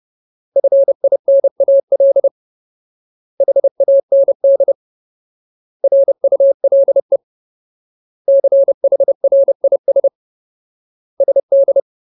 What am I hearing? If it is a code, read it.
Morse code, heard as a keyed tone, about 30 wpm, 550 Hz. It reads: FINAL HAND RULE CHRIS SD
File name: Ultimate Word List / 30WPM Ultimate